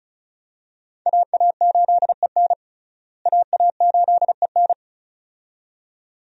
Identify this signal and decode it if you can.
Morse code, heard as a keyed tone, about 34 wpm, 700 Hz. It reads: AA8EN AA8EN
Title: USA Call Signs - 5 Characters - Repeated 2x - 35wpm